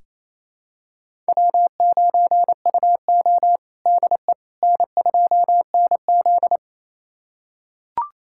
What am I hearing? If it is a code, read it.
Morse code, heard as a keyed tone, about 28 wpm, 700 Hz. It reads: W9UO DE N2NZ
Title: USA Callsign DE Callsign - 4 Characters - 28wpm